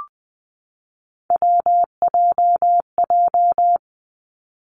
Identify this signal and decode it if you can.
Morse code, heard as a keyed tone, about 20 wpm, 700 Hz. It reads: WJJ
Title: Three Challenging Letter-Number Combinations 20wpm